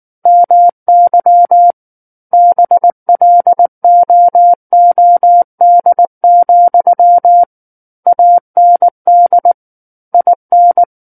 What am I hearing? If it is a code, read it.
Morse code, heard as a keyed tone, about 19 wpm, 700 Hz. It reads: MY BLOOD, AND IN
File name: Book One / 10-Chapter 10 - 19 WPM